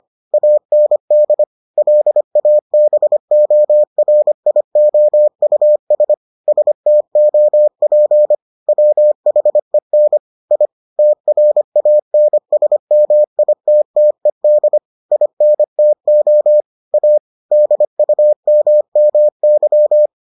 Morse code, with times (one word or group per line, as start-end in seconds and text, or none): 0.3-1.4 AND
1.8-6.1 LABORIOUS
6.5-8.3 STOP
8.7-10.2 WHEN
10.5-10.7 I
11.0-14.8 TRANSMITTED
15.1-16.6 INTO
16.9-17.2 A
17.5-20.1 DUMMY